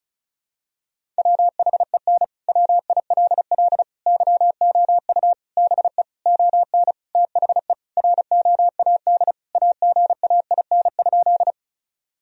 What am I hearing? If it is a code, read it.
Morse code, heard as a keyed tone, about 34 wpm, 700 Hz. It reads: WHEN WILL YOU BE ON THE ROAD AGAIN?
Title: Sentences from Top 1000 Words - Review Part 2 - 35wpm